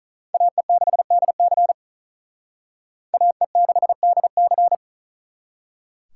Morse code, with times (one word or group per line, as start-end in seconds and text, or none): 0.3-1.7 AE6DC
3.1-4.8 AE6DC